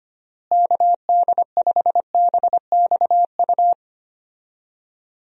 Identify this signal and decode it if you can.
Morse code, heard as a keyed tone, about 25 wpm, 700 Hz. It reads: KD5BXU
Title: USA Call Signs - 6 Characters - Repeated 3x - 25wpm